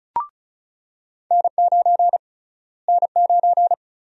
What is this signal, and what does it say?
Morse code, heard as a keyed tone, about 34 wpm, 700 Hz. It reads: N9 N9
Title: USA Call Sign Prefixes - Repeated 2x - 35wpm